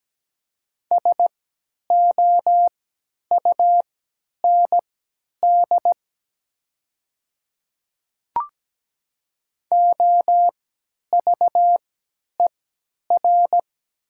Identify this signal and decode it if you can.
Morse code, heard as a keyed tone, about 17 wpm, spaced as for 10 wpm, 700 Hz. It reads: SOUND OVER
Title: 3x Character Spacing - Top 100 Words - 17wpm